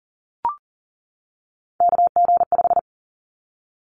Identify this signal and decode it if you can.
Morse code, heard as a keyed tone, about 40 wpm, 700 Hz. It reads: KG5
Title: Callsign Prefixes for Top 25 DXCC - 40wpm